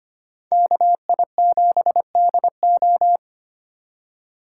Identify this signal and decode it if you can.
Morse code, heard as a keyed tone, about 32 wpm, 700 Hz. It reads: KI7DO